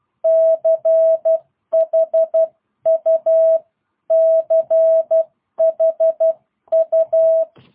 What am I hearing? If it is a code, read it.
Morse code, heard as a keyed tone, about 12 wpm, 660 Hz. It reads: CHU CHU
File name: jh8cbh / 2022cw